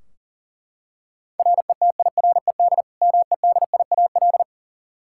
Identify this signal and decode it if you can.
Morse code, heard as a keyed tone, about 40 wpm, 700 Hz. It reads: RETIRED MEDIAL
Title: Practice QSO - Exchange 3 - More Info - 40wpm